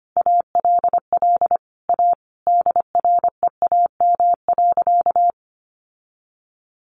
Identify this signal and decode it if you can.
Morse code, heard as a keyed tone, about 25 wpm, 700 Hz. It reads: ALL A DREAM.
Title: Sentences from Top 1500 Words 25wpm